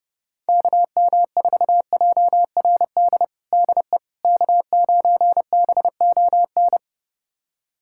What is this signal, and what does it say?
Morse code, heard as a keyed tone, about 30 wpm, 700 Hz. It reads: KM4JRD DE K9BON
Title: USA Callsign DE Callsign 30wpm